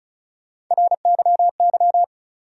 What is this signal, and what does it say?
Morse code, heard as a keyed tone, about 36 wpm, 700 Hz. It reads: RYY
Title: USA Call Sign Postfixes - 3 Letters 35wpm